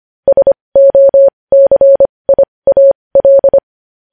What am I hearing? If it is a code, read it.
Morse code, heard as a keyed tone, about 25 wpm, spaced as for 20 wpm, 550 Hz. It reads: SOCIAL
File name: podcast / News headlines-1x-25x20